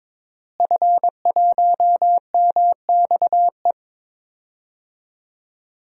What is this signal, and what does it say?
Morse code, heard as a keyed tone, about 22 wpm, 700 Hz. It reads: F1MXE
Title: DXCC-100 Call Signs - 5 Characters - Repeated 3x - 22wpm